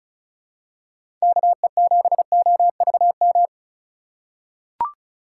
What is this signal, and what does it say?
Morse code, heard as a keyed tone, about 34 wpm, 700 Hz. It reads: KE7OVM
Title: USA Call Signs - 6 Characters - 4x Character Spacing - 35wpm